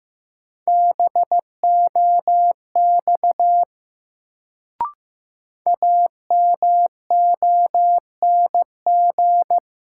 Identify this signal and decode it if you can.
Morse code, heard as a keyed tone, about 15 wpm, 700 Hz. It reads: BOX AMONG
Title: Sentences from Top 500 Words - New Word Drill - 15wpm